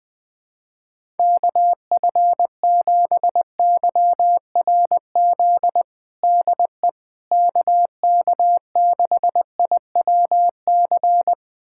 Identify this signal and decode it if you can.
Morse code, heard as a keyed tone, about 20 wpm, 700 Hz. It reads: KF7YRZ DE KK6IWC
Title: USA Callsign DE Callsign - 6 Characters - 20wpm